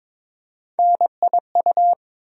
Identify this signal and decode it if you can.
Morse code, heard as a keyed tone, about 22 wpm, 700 Hz. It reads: NIU
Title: USA Call Sign Postfixes - 3 Letters 22wpm